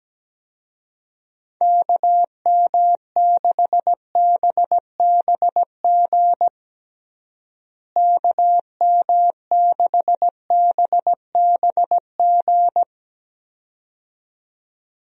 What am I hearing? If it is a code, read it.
Morse code, heard as a keyed tone, about 17 wpm, 700 Hz. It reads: KM6BBG KM6BBG
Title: USA Call Signs - 6 Characters - Repeated 2x - 17wpm